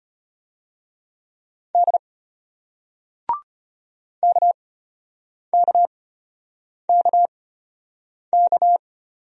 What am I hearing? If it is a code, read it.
Morse code, heard as a keyed tone, about 33 wpm, spaced as for 8 wpm, 700 Hz. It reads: D KKKK